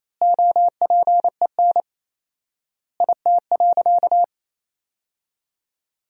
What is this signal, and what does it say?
Morse code, heard as a keyed tone, about 28 wpm, 700 Hz. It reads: OPEN IT.
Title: 4x Word-Spacing - Sentences from Top 300 Words - 28wpm